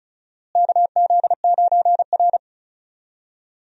Morse code, heard as a keyed tone, about 34 wpm, 700 Hz. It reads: KZ9R
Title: USA Call Signs - 4 Characters - Repeated 3x - 35wpm